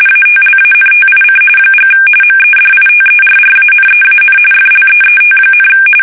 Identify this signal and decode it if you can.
RTTY, about 45.45 baud. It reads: ATLANTIC & MEDITERRANEAN FLEET,
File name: fsk850 / battleshipnj